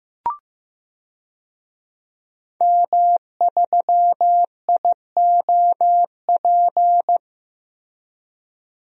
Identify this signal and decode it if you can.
Morse code, heard as a keyed tone, about 15 wpm, 700 Hz. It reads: M3IOP
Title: DXCC-100 Call Signs - 5 Characters - 15wpm